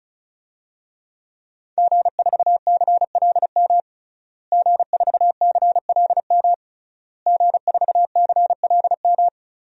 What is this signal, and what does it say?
Morse code, heard as a keyed tone, about 34 wpm, 700 Hz. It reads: G4CLM G4CLM G4CLM
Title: DXCC-100 Call Signs - 5 Characters - Repeated 3x - 35wpm